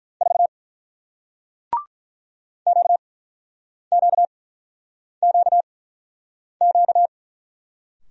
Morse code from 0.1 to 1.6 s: 4

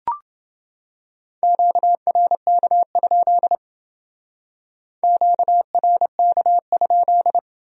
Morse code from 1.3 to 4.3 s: QRK?

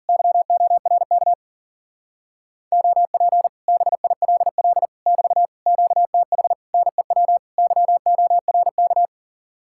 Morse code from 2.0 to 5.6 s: OP BILL <BT>

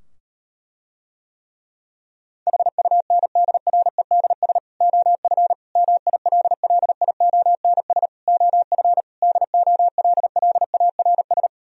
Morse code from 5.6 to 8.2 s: MILLIONS

